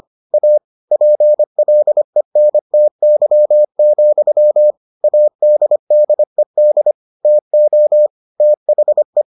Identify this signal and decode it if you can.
Morse code, heard as a keyed tone, about 25 wpm, 600 Hz. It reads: A PLENTY, ADDED TO THE